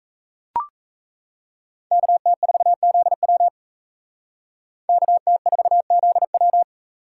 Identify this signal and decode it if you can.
Morse code, heard as a keyed tone, about 36 wpm, 700 Hz. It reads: KT4ZW KT4ZW